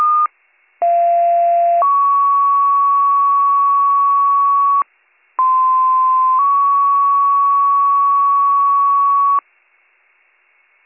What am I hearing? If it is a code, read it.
Two-tone page, heard as a one-second tone, then a three-second tone, about 690 then 1120 Hz, about 1040 then 1120 Hz. It is heard from 0.8 to 9.4 s.